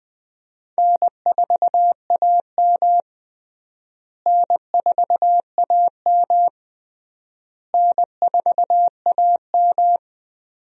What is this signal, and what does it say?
Morse code, heard as a keyed tone, about 19 wpm, 700 Hz. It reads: N4AM N4AM N4AM
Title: USA Call Signs - 4 Characters - Repeated 3x - 20wpm